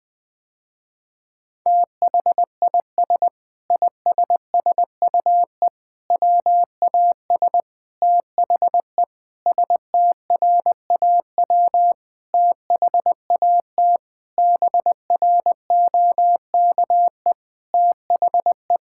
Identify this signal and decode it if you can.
Morse code, heard as a keyed tone, about 21 wpm, 700 Hz. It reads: THIS ISSUE WAS THE STRAW THAT BROKE THE
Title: Sentences with 7-Word Phrases - Spoken Beginning - 20wpm